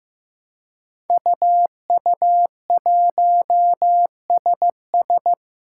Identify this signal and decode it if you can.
Morse code, heard as a keyed tone, about 15 wpm, 700 Hz. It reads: UU1SS
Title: Five Challenging Letter-Number Combinations 15wpm